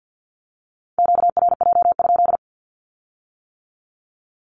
Morse code, heard as a keyed tone, about 50 wpm, 700 Hz. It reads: QRJ?
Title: Q-Codes with Spoken Meaning - 50wpm